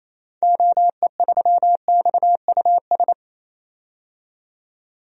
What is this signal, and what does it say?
Morse code, heard as a keyed tone, about 28 wpm, 700 Hz. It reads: OE3XUS
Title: DXCC-100 Call Signs - 6 Characters - 28wpm